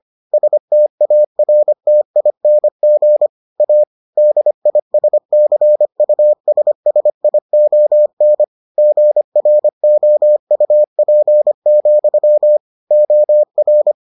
Morse code, heard as a keyed tone, about 25 wpm, 600 Hz. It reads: STARTING A DISCUSSION GROUP, OR